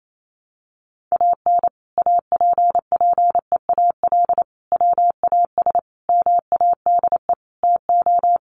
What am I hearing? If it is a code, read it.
Morse code, heard as a keyed tone, about 28 wpm, 700 Hz. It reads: AN APPEAL WAS MADE TO
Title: Sentences from Top 1300 Words 28wpm